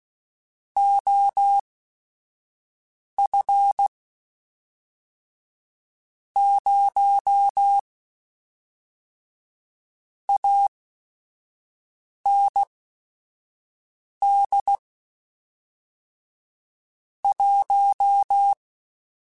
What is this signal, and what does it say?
Morse code, heard as a keyed tone, about 16 wpm, spaced as for 5 wpm, 800 Hz. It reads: OF 0 AND 1